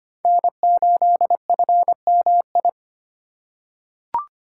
Morse code, heard as a keyed tone, about 25 wpm, 700 Hz. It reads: N8FMI
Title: USA Call Signs - 5 Characters - 4x Character Spacing - 25wpm